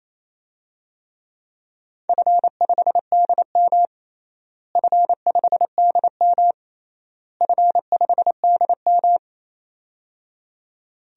Morse code, heard as a keyed tone, about 28 wpm, 700 Hz. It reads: F5DM F5DM F5DM
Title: DXCC-100 Call Signs - 4 Characters - Repeated 3x - 28wpm